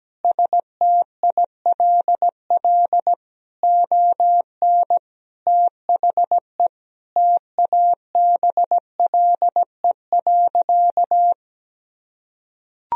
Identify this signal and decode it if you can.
Morse code, heard as a keyed tone, about 17 wpm, 700 Hz. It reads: STILL ON THE TABLE.
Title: Sentences from Top 1000 Words - Review Part 4 - 17wpm